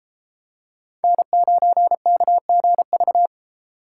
Morse code, heard as a keyed tone, about 29 wpm, 700 Hz. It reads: N9KGV